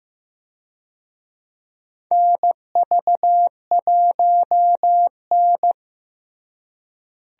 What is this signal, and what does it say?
Morse code, heard as a keyed tone, about 15 wpm, 700 Hz. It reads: NV1N
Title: USA Call Signs - 4 Characters - Repeated 3x - 15wpm